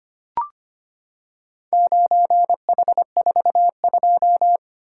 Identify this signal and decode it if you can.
Morse code, heard as a keyed tone, about 25 wpm, 700 Hz. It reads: 9H42
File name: practice / 25wpm / Four Letter-Number Combinations - 25wpm